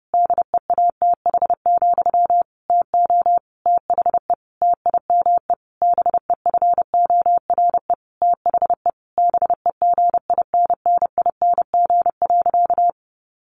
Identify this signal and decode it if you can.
Morse code, heard as a keyed tone, about 30 wpm, 700 Hz. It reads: DEATH, TO THE TIME BEFORE THE BEGINNING.
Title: A Sound Of Thunder 30wpm